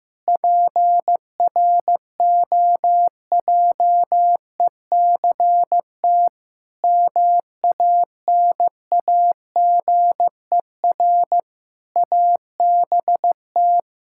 Morse code, heard as a keyed tone, about 15 wpm, 700 Hz. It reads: PROJECT MANAGER ABT
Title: Practice QSO - Exchange 3 - More Info - 15wpm